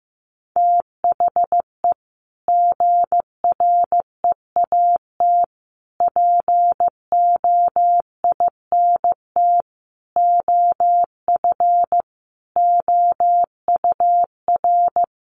Morse code, heard as a keyed tone, about 15 wpm, 700 Hz. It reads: THE GREAT POINT OF OUR
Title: Sentences from Top 1500 Words 15wpm